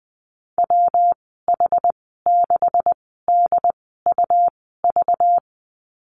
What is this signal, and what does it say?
Morse code, heard as a keyed tone, about 20 wpm, spaced as for 15 wpm, 700 Hz. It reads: WH6DUV
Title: 238 Lesson 38g - Letter q - Call Signs 20wpm